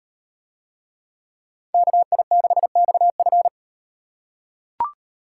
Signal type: Morse code

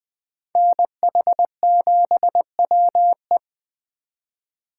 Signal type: Morse code

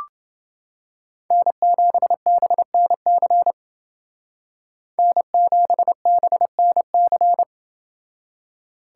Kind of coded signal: Morse code

